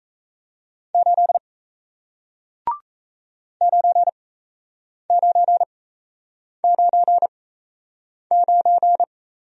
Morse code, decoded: 8 9999

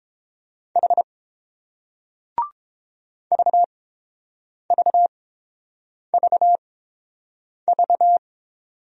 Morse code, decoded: H VVVV

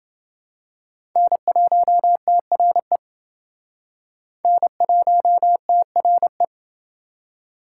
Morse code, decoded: N1TRE N1TRE

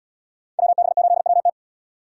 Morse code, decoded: WB8WT

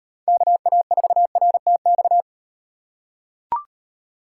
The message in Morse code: KA4RTX

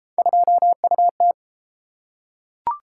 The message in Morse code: 2UT

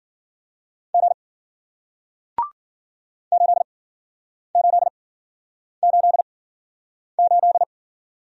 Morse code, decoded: G 8888

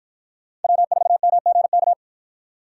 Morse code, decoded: W4MGK

